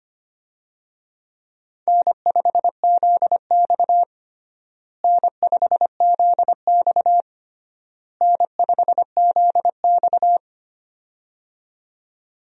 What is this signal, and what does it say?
Morse code, heard as a keyed tone, about 25 wpm, 700 Hz. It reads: N5ZX N5ZX N5ZX
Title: USA Call Signs - 4 Characters - Repeated 3x - 25wpm